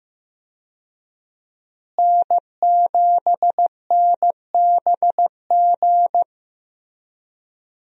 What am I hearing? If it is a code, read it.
Morse code, heard as a keyed tone, about 16 wpm, 700 Hz. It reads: N7NBG
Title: USA Call Signs - 5 Characters - Repeated 2x - 15wpm